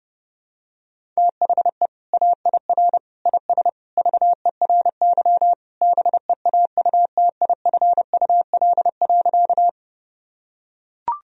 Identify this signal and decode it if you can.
Morse code, heard as a keyed tone, about 32 wpm, 700 Hz. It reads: THE AIR IS VERY BEAUTIFUL.
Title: Sentences from Top 1000 Words - Review Part 1 - 30wpm